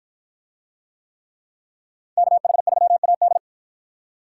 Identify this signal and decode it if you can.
Morse code, heard as a keyed tone, about 53 wpm, 700 Hz. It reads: KH3AD